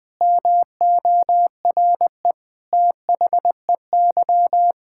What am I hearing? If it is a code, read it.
Morse code, heard as a keyed tone, about 21 wpm, 700 Hz. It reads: MORE THEY
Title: Sets of 8 Words - 20wpm